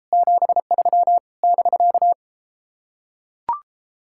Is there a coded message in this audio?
Yes: Morse code from 0.1 to 2.1 s, heard as a keyed tone, about 33 wpm, 700 Hz. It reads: 73 <BK>